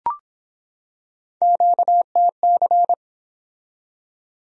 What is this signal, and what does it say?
Morse code, heard as a keyed tone, about 23 wpm, 700 Hz. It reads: QTC